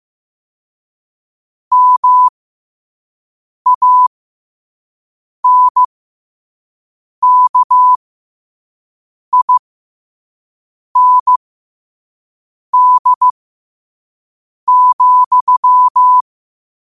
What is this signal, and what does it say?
Morse code, heard as a keyed tone, about 15 wpm, spaced as for 5 wpm, 1000 Hz. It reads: MANKIND,
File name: qotdfiles / 5wpm